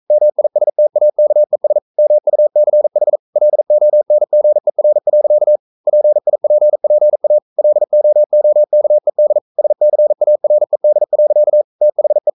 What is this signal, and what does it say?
Morse code, heard as a keyed tone, about 42 wpm, 600 Hz. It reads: MISTAKES MUCH LONGER. PIPPA LOOKED SCARED. THE
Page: Romanen Never af Ken Follet på engelsk.